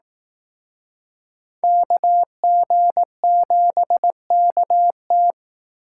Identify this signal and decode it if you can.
Morse code, heard as a keyed tone, about 18 wpm, 700 Hz. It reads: KG7KT
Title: Speed Racing - USA 2x2 Call Signs 15wpm